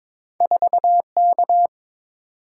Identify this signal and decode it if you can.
Morse code, heard as a keyed tone, about 30 wpm, 700 Hz. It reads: 4K